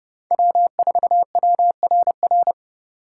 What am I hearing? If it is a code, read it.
Morse code, heard as a keyed tone, about 34 wpm, 700 Hz. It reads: W4WRR
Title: Speed Racing - USA 1x3 Call Signs 30wpm